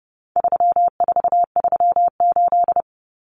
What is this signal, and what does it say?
Morse code, heard as a keyed tone, about 30 wpm, 700 Hz. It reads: 3438
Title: Numbers - 4-Digit 30wpm